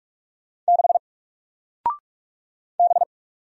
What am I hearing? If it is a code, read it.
Morse code, heard as a keyed tone, about 44 wpm, 700 Hz. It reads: 6 B
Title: Single Character (Letter:Number:Punctuation:Prosign) 45wpm